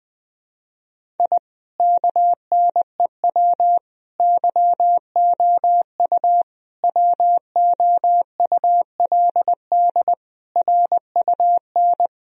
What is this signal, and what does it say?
Morse code, heard as a keyed tone, about 20 wpm, 700 Hz. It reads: I KNEW YOU WOULD RUN
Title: Sentences from Top 1000 Words - Review Part 1 - 20wpm